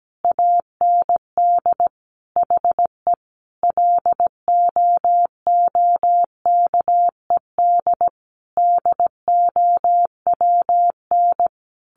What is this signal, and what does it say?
Morse code, heard as a keyed tone, about 17 wpm, 700 Hz. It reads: AND HE LOOKED DOWN
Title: A Sound Of Thunder 17wpm